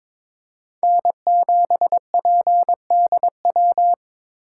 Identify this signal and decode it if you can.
Morse code, heard as a keyed tone, about 22 wpm, 700 Hz. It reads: N7PDW
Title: USA Call Signs - 5 Characters - 2x Character Spacing - 22wpm